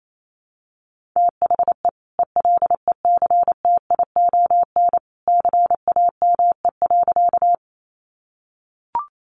Morse code, heard as a keyed tone, about 28 wpm, 700 Hz. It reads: THE ELECTION CAME.
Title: Sentences from Top 1500 Words 28wpm